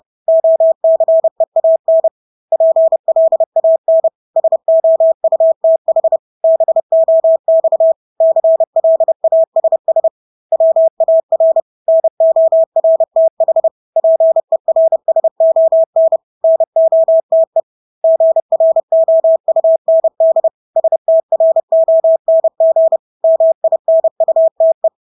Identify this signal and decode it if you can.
Morse code, heard as a keyed tone, about 30 wpm, 650 Hz. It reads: OCEAN PLAN SOUTH BOX CLASS WAR NORTH PERSON NOTE GROUND STRONG MINUTE
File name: Engelske ord 30-30 wpm